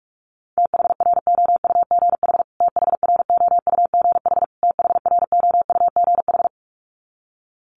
Morse code, heard as a keyed tone, about 44 wpm, 700 Hz. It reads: THROUGH THROUGH THROUGH
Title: Top 500 Words Repeated 45wpm